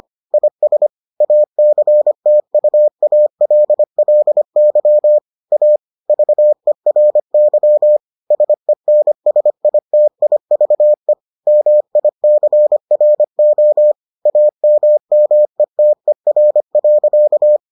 Morse code, heard as a keyed tone, about 25 wpm, 600 Hz. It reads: IS ACTUALLY A VERY SENSITIVE MICRO AMMETER.